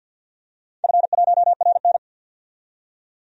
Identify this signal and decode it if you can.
Morse code, heard as a keyed tone, about 49 wpm, 700 Hz. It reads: F1RN
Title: DXCC-100 Call Signs - 4 Characters - 50wpm